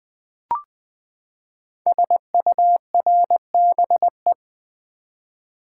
Morse code, heard as a keyed tone, about 20 wpm, 700 Hz. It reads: SURBE
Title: Code Groups - 5 Letters - 20wpm